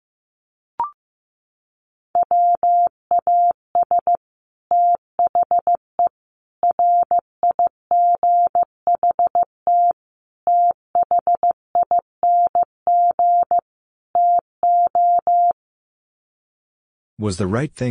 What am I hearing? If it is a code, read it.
Morse code, heard as a keyed tone, about 14 wpm, 700 Hz. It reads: WAS THE RIGHT THING TO
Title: Sets of 5 Words Encore2 15wpm